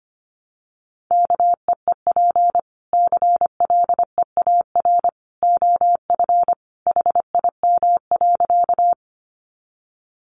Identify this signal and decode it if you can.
Morse code, heard as a keyed tone, about 25 wpm, 700 Hz. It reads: KEEP CLEAR OF HIM.
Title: Sentences from Top 500 Words 25wpm